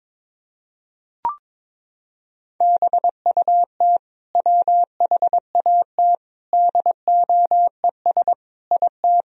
Morse code, heard as a keyed tone, about 21 wpm, 700 Hz. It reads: BUT WHAT DOES IT
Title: Sentences from Top 1000 Words - Review Part 4 - 22wpm